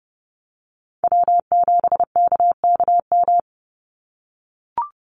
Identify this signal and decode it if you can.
Morse code, heard as a keyed tone, about 30 wpm, 700 Hz. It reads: W7KKM
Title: NA SOTA Chaser Callsigns 30wpm